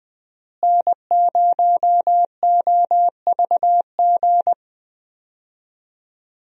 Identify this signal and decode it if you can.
Morse code, heard as a keyed tone, about 20 wpm, 700 Hz. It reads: N0OVG
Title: USA Call Signs - 5 Characters - Repeated 3x - 20wpm